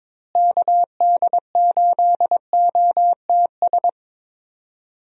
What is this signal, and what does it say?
Morse code, heard as a keyed tone, about 22 wpm, 700 Hz. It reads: KD8OTS
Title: USA Call Signs - 6 Characters - Repeated 3x - 22wpm